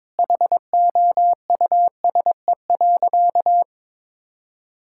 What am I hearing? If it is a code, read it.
Morse code, heard as a keyed tone, about 21 wpm, 700 Hz. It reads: HOUSE.
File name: Difficult Sentences with Hints - 22wpm